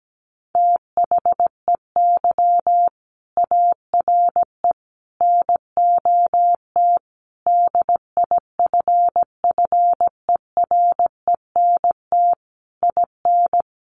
Morse code, heard as a keyed tone, about 17 wpm, 700 Hz. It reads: THEY ARE NOT DIFFERENT IN
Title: Sentences from Top 1400 Words 17wpm